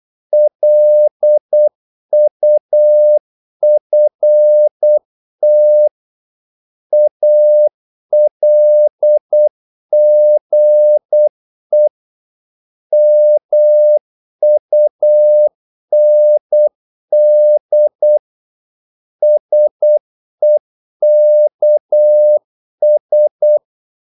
Morse code, heard as a keyed tone, about 8 wpm, 600 Hz. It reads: LUFT ALGE MUND SEKS
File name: Korte ord DK 8wpm